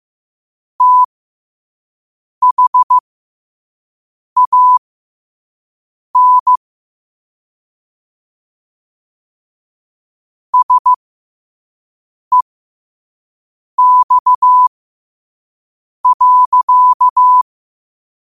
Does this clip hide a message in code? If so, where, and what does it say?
Yes: Morse code from 0.8 to 17.4 s, heard as a keyed tone, about 15 wpm, spaced as for 5 wpm, 1000 Hz. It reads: THAN SEX.